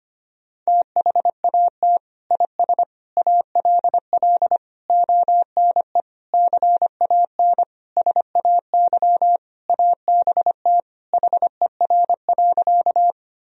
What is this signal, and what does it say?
Morse code, heard as a keyed tone, about 25 wpm, 700 Hz. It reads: THAT IS ALL ONE CAN SAY ABT HER.
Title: Sentences from Top 200 Words and Abbreviations 25wpm